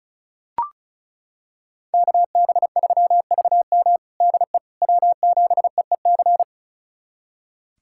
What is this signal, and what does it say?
Morse code, heard as a keyed tone, about 36 wpm, 700 Hz. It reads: KB3VM DE W7EEC
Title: USA Callsign DE Callsign - 5 Characters - 35wpm